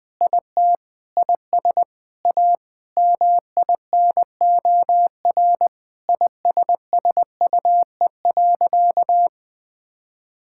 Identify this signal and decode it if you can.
Morse code, heard as a keyed tone, about 19 wpm, 700 Hz. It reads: IT IS A MINOR ISSUE.
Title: Sentences with 6-Word Phrases - Spoken Beginning - 20wpm